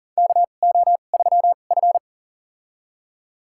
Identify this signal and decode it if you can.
Morse code, heard as a keyed tone, about 40 wpm, spaced as for 29 wpm, 700 Hz. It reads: KO3F
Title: USA Call Signs - 4 Characters - 2x Character Spacing - 40wpm